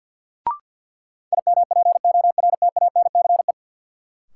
Morse code, heard as a keyed tone, about 51 wpm, 700 Hz. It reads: IMPORTANCE
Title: Sentences from Top 1200 Words - New Word Drill - 50wpm